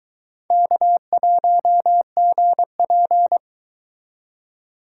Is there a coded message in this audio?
Yes: Morse code from 0.5 to 3.4 s, heard as a keyed tone, about 22 wpm, 700 Hz. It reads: K1GP